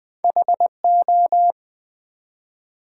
Morse code, heard as a keyed tone, about 20 wpm, 700 Hz. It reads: HO
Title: Build A Word - 1 Letter At A Time - 4-Letter Words - 20wpm